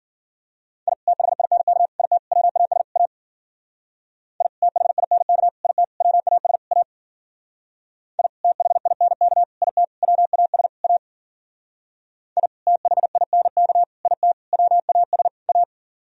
Morse code, heard as a keyed tone, about 51 wpm, 700 Hz. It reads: I THINK IT WAS A I THINK IT WAS A I THINK IT WAS A I THINK IT WAS A